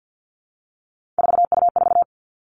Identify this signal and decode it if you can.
Morse code, heard as a keyed tone, about 50 wpm, 700 Hz. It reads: 4U4
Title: Three Challenging Letter-Number Combinations 50wpm